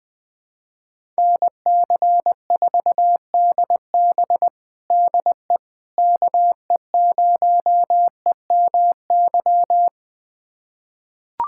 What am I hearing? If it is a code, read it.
Morse code, heard as a keyed tone, about 21 wpm, 700 Hz. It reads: NC4DB DE KE0EMY
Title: USA Callsign DE Callsign 20wpm